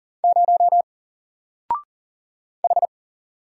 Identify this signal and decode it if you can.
Morse code, heard as a keyed tone, about 40 wpm, 700 Hz. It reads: 0 H
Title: Single Letter-Number Alt 40wpm